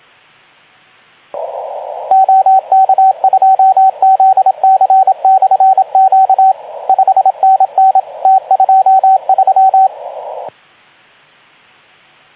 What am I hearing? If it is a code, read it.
Morse code, heard as a keyed tone, about 25 wpm, 750 Hz, with radio noise. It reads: OK2ZC/Q 5NN T23
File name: Archiv / CUC189 / ok2zcq cuc 189